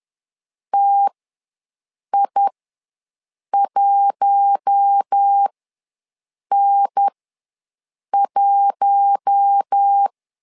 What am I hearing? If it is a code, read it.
Morse code, heard as a keyed tone, about 10 wpm, spaced as for 6 wpm, 800 Hz. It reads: TI1N1
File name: LEZIONI / 4 Q1 ws 10wpm 2538